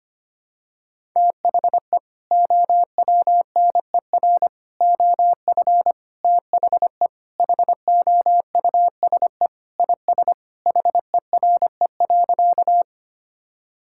Morse code, heard as a keyed tone, about 25 wpm, 700 Hz. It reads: THE OWNER OF THE HOUSE IS HERE.
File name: practice / 25wpm / Sentences with 3-Word Phrases - 25wpm